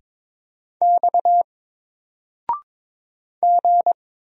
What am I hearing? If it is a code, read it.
Morse code, heard as a keyed tone, about 22 wpm, 700 Hz. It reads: X G
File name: Single-Letter-Number-22wpm